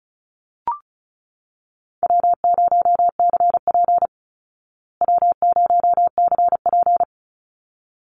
Morse code, heard as a keyed tone, about 34 wpm, 700 Hz. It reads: W0CP W0CP
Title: Easy Call Signs Repeated 2x - 35wpm